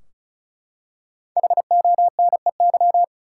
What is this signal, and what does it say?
Morse code, heard as a keyed tone, about 34 wpm, 700 Hz. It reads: HONEY
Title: Five Letter Words - 35wpm